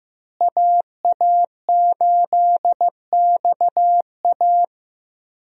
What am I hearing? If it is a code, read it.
Morse code, heard as a keyed tone, about 15 wpm, 700 Hz. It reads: AA8XA
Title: USA Call Signs - 5 Characters - Repeated 3x - 15wpm